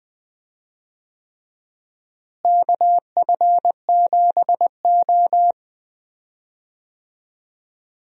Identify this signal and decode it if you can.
Morse code, heard as a keyed tone, about 20 wpm, 700 Hz. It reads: KF7O
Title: USA Call Signs - 4 Characters - Repeated 3x - 20wpm